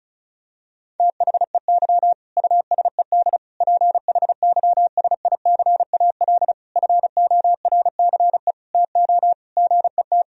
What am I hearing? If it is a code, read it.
Morse code, heard as a keyed tone, about 34 wpm, 700 Hz. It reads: THEY USED PHYSICAL FORCE TO GET
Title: Sentences from Top 1000 Words - Review Part 4 - 35wpm